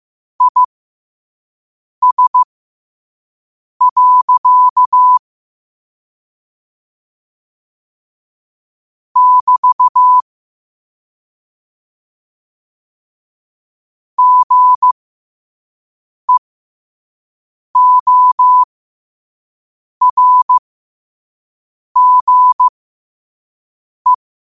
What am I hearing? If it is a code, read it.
Morse code, heard as a keyed tone, about 15 wpm, spaced as for 5 wpm, 1000 Hz. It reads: IS. <BT> GEORGE
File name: qotdfiles / 5wpm